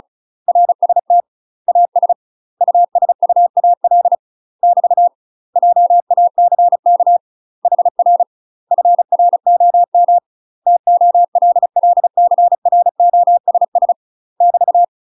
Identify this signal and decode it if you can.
Morse code, heard as a keyed tone, about 36 wpm, 700 Hz. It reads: RST AS USUAL <BT> JACK HR FROM TOLLCROSS <BT>
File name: UK MASTER QSOs TRAINING / 35WPM / UK MASTER QSOs 0029